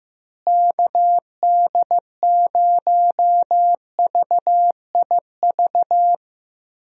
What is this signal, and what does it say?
Morse code, heard as a keyed tone, about 15 wpm, 700 Hz. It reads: KD0VIV
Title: USA Call Signs - 6 Characters - Repeated 3x - 15wpm